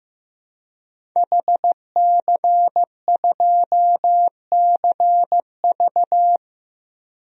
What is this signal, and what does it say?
Morse code, heard as a keyed tone, about 15 wpm, 700 Hz. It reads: HC2CV
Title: DXCC-100 Call Signs - 5 Characters - 15wpm